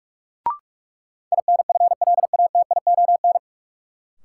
Morse code, heard as a keyed tone, about 42 wpm, 700 Hz. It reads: INFLATION
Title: Sentences from Top 1100 Words - New Word Drill Alt - 45wpm